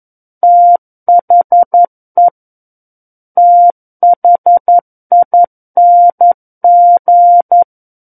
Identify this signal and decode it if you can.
Morse code, heard as a keyed tone, about 11 wpm, 700 Hz. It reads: THE THING
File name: 11 WPM